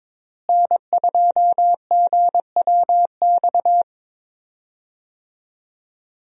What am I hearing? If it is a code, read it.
Morse code, heard as a keyed tone, about 22 wpm, 700 Hz. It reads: N2GWX
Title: USA Call Signs - 5 Characters - Repeated 2x - 22wpm